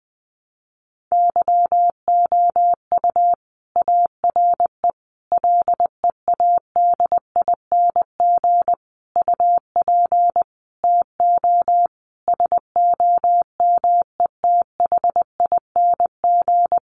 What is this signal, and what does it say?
Morse code, heard as a keyed tone, about 20 wpm, 700 Hz. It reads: YOU ARE LEADING UP TO SOMETHING
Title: Sentences from Top 1100 Words 20wpm